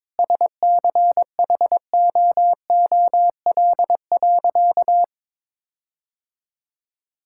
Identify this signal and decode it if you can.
Morse code, heard as a keyed tone, about 22 wpm, 700 Hz. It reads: SCHOOL.
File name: practice / 22wpm / Sentences with 4-Word Phrases - 22wpm